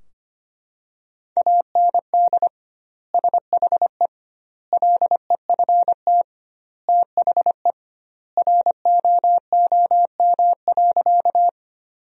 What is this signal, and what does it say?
Morse code, heard as a keyed tone, about 25 wpm, 700 Hz. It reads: AND SHE LEFT THE ROOM.
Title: 2x Word-Spacing - Sentences from Top 300 Words - 25wpm